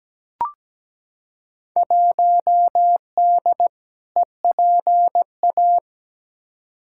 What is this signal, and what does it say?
Morse code, heard as a keyed tone, about 17 wpm, 700 Hz. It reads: 1D EPA
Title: Field Day Exchanges - 17wpm